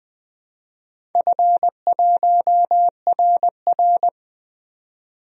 Morse code, heard as a keyed tone, about 20 wpm, 700 Hz. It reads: F1RR
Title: DXCC-100 Call Signs - 4 Characters - 20wpm